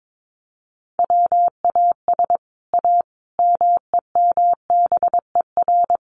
Morse code, heard as a keyed tone, about 22 wpm, 700 Hz. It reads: WAS A MEMBER
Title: Sets of 5 Words 22wpm